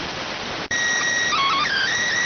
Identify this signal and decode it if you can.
SSTV (slow-scan television), PD 290.